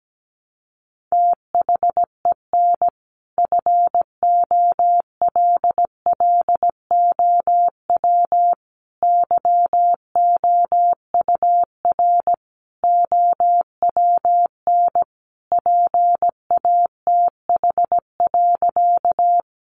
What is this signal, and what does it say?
Morse code, heard as a keyed tone, about 17 wpm, 700 Hz. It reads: THEN FOLLOW YOUR OWN PATH.
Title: Sentences from Top 1400 Words 17wpm